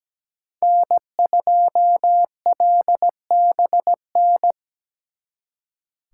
Morse code, heard as a keyed tone, about 20 wpm, 700 Hz. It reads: N2LBN
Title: Speed Racing - USA 1x3 Call Signs 17wpm